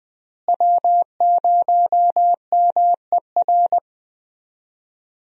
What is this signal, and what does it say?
Morse code, heard as a keyed tone, about 20 wpm, 700 Hz. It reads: W0MER
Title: USA Call Signs - 5 Characters - Repeated 3x - 20wpm